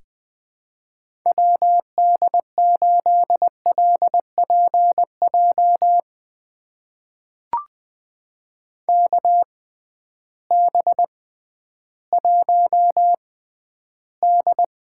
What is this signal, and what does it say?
Morse code, heard as a keyed tone, about 20 wpm, spaced as for 12 wpm, 700 Hz. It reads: WD8LPJ KB1D
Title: USA Call Signs - 6 Characters - 6x Character Spacing - 20wpm